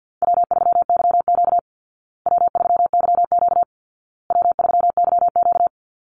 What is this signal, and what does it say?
Morse code, heard as a keyed tone, about 48 wpm, 700 Hz. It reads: W3YQ W3YQ W3YQ
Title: Easy Call Signs Repeated 3x - 50wpm